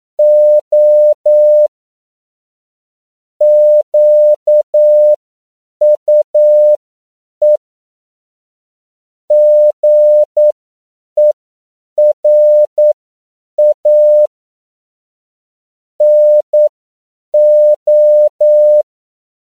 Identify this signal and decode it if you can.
Morse code, heard as a keyed tone, about 9 wpm, spaced as for 7 wpm, 600 Hz. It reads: O QUE GERA NO
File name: CW / 10PPM